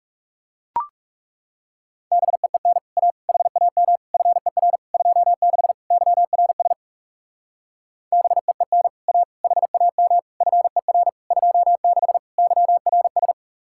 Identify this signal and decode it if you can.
Morse code, heard as a keyed tone, about 36 wpm, 700 Hz. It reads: BEEN A HAM FER 26 YRS BEEN A HAM FER 26 YRS
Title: Speed-Racing Common QSO Phrases - 30wpm